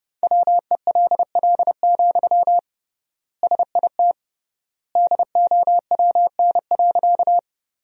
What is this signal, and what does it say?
Morse code, heard as a keyed tone, about 31 wpm, 700 Hz. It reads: WELL, SIT DOWN.
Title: 3x Word-Spacing - Sentences from Top 500 Words - 30wpm